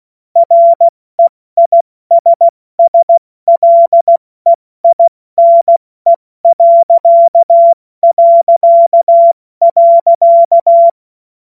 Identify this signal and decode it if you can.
Morse code, heard as a keyed tone, about 16 wpm, 680 Hz. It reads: REISSLEINE...
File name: Morsekurs / Witze / Witz 02. Dezember 2025 70 F80